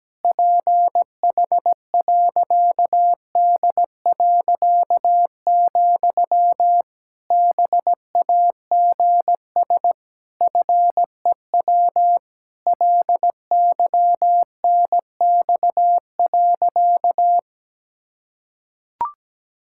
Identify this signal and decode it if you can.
Morse code, heard as a keyed tone, about 17 wpm, 700 Hz. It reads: PH.D., BAGS FEW LYNX.
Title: Pangram Challenge v2 - 17wpm